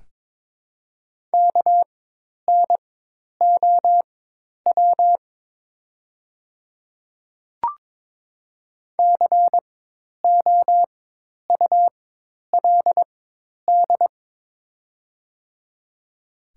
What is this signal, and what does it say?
Morse code, heard as a keyed tone, about 22 wpm, spaced as for 10 wpm, 700 Hz. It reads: KNOW COULD
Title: 4x Character Spacing - Top 100 Words - 22wpm